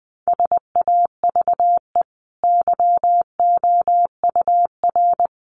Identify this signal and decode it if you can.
Morse code, heard as a keyed tone, about 20 wpm, 700 Hz. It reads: SAVE YOUR
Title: Sentences from Top 1300 Words 20wpm